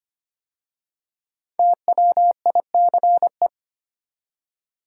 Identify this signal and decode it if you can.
Morse code, heard as a keyed tone, about 25 wpm, 700 Hz. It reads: TWICE
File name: 5 Letter Words - First 3 Letters Given - 25wpm